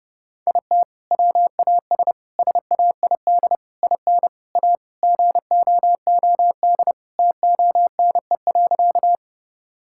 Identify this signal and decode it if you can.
Morse code, heard as a keyed tone, about 30 wpm, 700 Hz. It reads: IT WAS SAID IN A GOOD TONE.
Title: Sentences from Top 1600 Words 30wpm